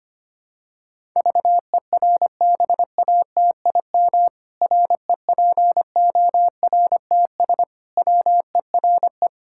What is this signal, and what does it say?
Morse code, heard as a keyed tone, about 25 wpm, 700 Hz. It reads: VERBATIM REPORTS WERE
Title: Pangram Challenge v2 - 25wpm